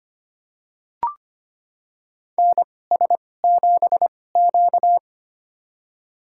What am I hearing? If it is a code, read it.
Morse code, heard as a keyed tone, about 25 wpm, spaced as for 18 wpm, 700 Hz. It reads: NS7Q
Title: USA Call Signs - 4 Characters - 2x Character Spacing - 25wpm